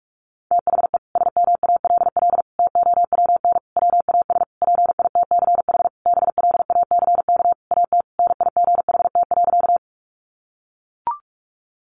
Morse code, heard as a keyed tone, about 44 wpm, 700 Hz. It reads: THE SMALL TOWN WAS PITCH BLACK AT NIGHT.
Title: Sentences from Top 1300 Words 45wpm